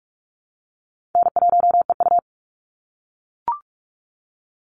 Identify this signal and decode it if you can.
Morse code, heard as a keyed tone, about 44 wpm, 700 Hz. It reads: N1EU
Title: NA SOTA Chaser Callsigns 45wpm